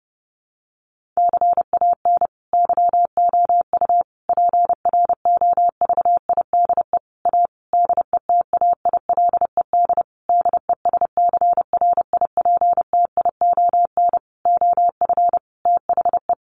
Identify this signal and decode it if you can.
Morse code, heard as a keyed tone, about 30 wpm, 700 Hz. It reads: CAN YOU PROVIDE A DETAILED DESCRIPTION OF THE
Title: Sentences from Top 1500 Words 30wpm